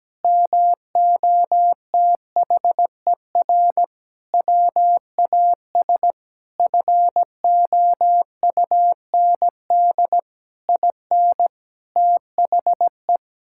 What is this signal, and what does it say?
Morse code, heard as a keyed tone, about 17 wpm, 700 Hz. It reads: MOTHER WAS FOUND IN THE
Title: Sentences from Top 1000 Words - Review Part 5 - 17wpm